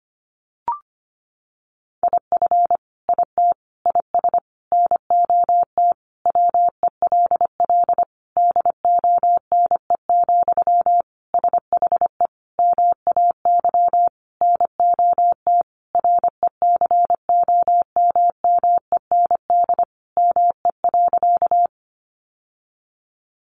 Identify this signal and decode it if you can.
Morse code, heard as a keyed tone, about 25 wpm, 700 Hz. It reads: IF IT IS NOT WELL DONE, SHE MAY NOT RECOMMEND ME.
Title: Sentences from Top 1200 Words 25wpm